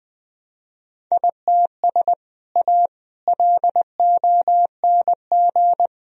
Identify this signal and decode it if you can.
Morse code, heard as a keyed tone, about 20 wpm, 700 Hz. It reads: ITS A LONG
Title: Sentences from Top 1000 Words - Review Part 1 - 20wpm